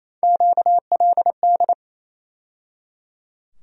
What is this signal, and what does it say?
Morse code, heard as a keyed tone, about 28 wpm, 700 Hz. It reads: QLD
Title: USA Call Sign Postfixes - 3 Letters 28wpm